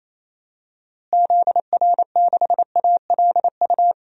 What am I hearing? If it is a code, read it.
Morse code, heard as a keyed tone, about 28 wpm, 700 Hz. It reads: ZR6ALU
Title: DXCC-100 Call Signs - 6 Characters - Repeated 3x - 28wpm